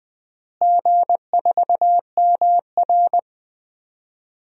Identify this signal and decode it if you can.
Morse code, heard as a keyed tone, about 20 wpm, 700 Hz. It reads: G4MR
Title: DXCC-100 Call Signs - 4 Characters - Repeated 2x - 20wpm